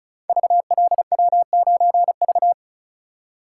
Morse code, heard as a keyed tone, about 36 wpm, 700 Hz. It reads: VLW9V
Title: Code Groups - 5 Letter:Numbers - 35wpm